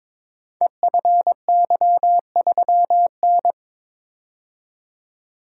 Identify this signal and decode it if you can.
Morse code, heard as a keyed tone, about 22 wpm, 700 Hz. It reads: EFY3N
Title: Code Groups - 5 Letter:Numbers - 22wpm